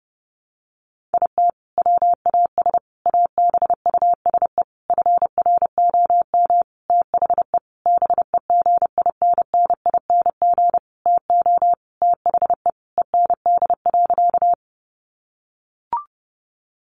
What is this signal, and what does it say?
Morse code, heard as a keyed tone, about 30 wpm, 700 Hz. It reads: IT WAS ABUSE FROM THE BEGINNING TO THE END.
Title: Sentences from Top 1300 Words 30wpm